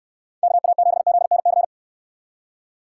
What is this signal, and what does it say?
Morse code, heard as a keyed tone, about 69 wpm, 700 Hz. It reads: KA7ZNC